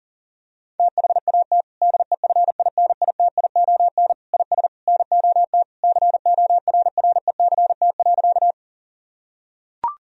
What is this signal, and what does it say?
Morse code, heard as a keyed tone, about 40 wpm, 700 Hz. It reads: THAT DEFINITION IS NOT CORRECT.
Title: Sentences from Top 1000 Words - Review Part 4 - 40wpm